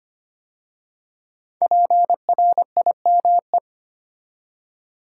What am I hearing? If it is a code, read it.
Morse code, heard as a keyed tone, about 25 wpm, 700 Hz. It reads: PRIME
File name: 5 Letter Words - First 3 Letters Given - 25wpm